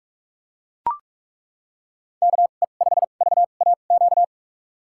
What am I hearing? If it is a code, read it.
Morse code, heard as a keyed tone, about 42 wpm, spaced as for 31 wpm, 700 Hz. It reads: KE5VAQ
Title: USA Call Signs - 6 Characters - 2x Character Spacing - 45wpm